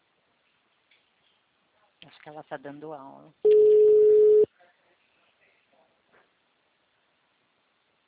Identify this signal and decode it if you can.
Ringback tone (Europe, 425 Hz), with speech.
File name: out- -2008- - - .208